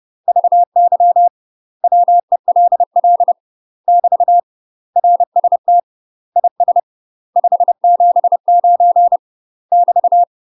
Morse code, heard as a keyed tone, about 30 wpm, 700 Hz. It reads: VY WELL <BT> RST IS 579 <BT>
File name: UK MASTER QSOs TRAINING/30WPM